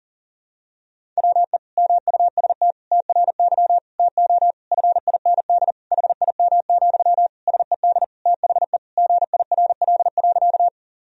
Morse code, heard as a keyed tone, about 40 wpm, 700 Hz. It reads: WE MUST TRY TO FIND HIM, SED THE GIRL.
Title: Sentences from Top 400 Words and Abbreviations 40wpm